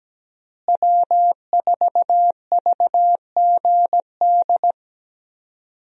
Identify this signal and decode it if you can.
Morse code, heard as a keyed tone, about 17 wpm, 700 Hz. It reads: W4VGD
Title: USA Call Signs - 5 Characters - Repeated 3x - 17wpm